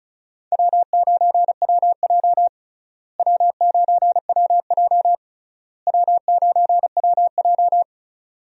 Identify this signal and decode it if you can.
Morse code, heard as a keyed tone, about 36 wpm, 700 Hz. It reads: W9WJ W9WJ W9WJ
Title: USA Call Signs - 4 Characters - Repeated 3x - 35wpm